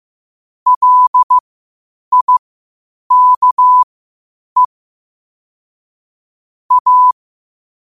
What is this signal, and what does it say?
Morse code, heard as a keyed tone, about 15 wpm, spaced as for 8 wpm, 1000 Hz. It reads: LIKE A
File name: qotdfiles / 8wpm